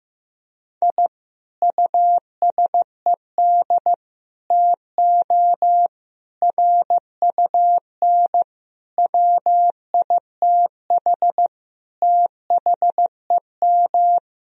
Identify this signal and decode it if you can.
Morse code, heard as a keyed tone, about 15 wpm, 700 Hz. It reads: I USED TO RUN WITH THEM
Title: Sentences from Top 1000 Words - Review Part 4 - 15wpm